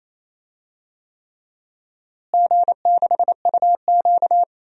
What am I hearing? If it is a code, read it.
Morse code, heard as a keyed tone, about 28 wpm, 700 Hz. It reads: G6UQ
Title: DXCC-100 Call Signs - 4 Characters - Repeated 3x - 28wpm